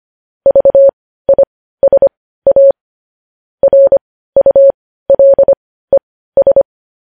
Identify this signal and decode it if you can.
Morse code, heard as a keyed tone, about 25 wpm, spaced as for 15 wpm, 550 Hz. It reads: VISA RULES
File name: podcast / News headlines-1x-25x15